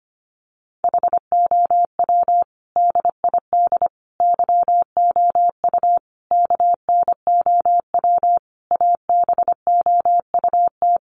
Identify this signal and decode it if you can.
Morse code, heard as a keyed tone, about 25 wpm, 700 Hz. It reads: HOW DID YOU KNOW ABOUT
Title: Sentences from Top 100 Words 25wpm